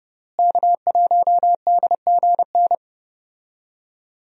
Morse code, heard as a keyed tone, about 30 wpm, 700 Hz. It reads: K1DGN
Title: USA Call Signs - 5 Characters - Repeated 2x - 30wpm